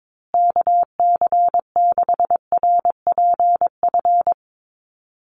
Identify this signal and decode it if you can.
Morse code, heard as a keyed tone, about 22 wpm, 700 Hz. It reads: KC6RPF
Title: USA Call Signs Repeated 3x 22wpm